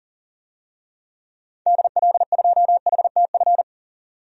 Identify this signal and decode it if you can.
Morse code, heard as a keyed tone, about 40 wpm, 700 Hz. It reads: DL2HTF
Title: DXCC-100 Call Signs - 6 Characters - 40wpm